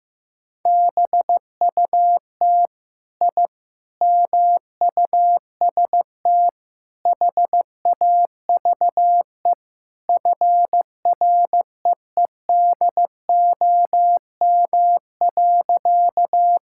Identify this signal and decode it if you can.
Morse code, heard as a keyed tone, about 15 wpm, 700 Hz. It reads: BUT I MUST HAVE FREEDOM.
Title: Sentences from Top 1000 Words - Review Part 5 - 15wpm